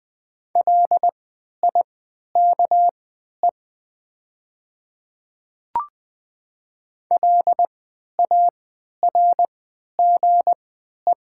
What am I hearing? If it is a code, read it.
Morse code, heard as a keyed tone, about 20 wpm, spaced as for 11 wpm, 700 Hz. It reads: LIKE LARGE
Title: 3x Character Spacing - Top 200 Words - 20wpm